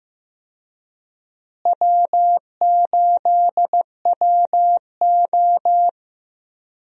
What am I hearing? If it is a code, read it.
Morse code, heard as a keyed tone, about 15 wpm, 700 Hz. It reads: W8WO
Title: USA Call Signs - 4 Characters - Repeated 2x - 15wpm